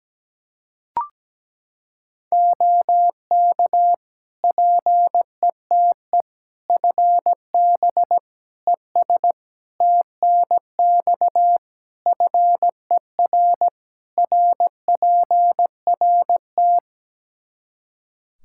Morse code, heard as a keyed tone, about 17 wpm, 700 Hz. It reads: OK PETE FB ES TNX FER RPRT
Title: Practice QSO - Exchange 2 - Info - 17wpm